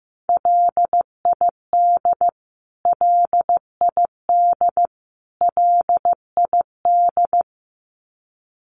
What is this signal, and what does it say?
Morse code, heard as a keyed tone, about 15 wpm, 700 Hz. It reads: LID LID LID
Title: Abbreviations Repeated 3x - 15wpm